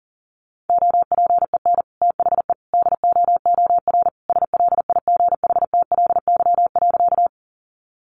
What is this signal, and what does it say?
Morse code, heard as a keyed tone, about 40 wpm, 700 Hz. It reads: OPEN THE DOOR SLIGHTLY.
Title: Sentences from Top 1300 Words 40wpm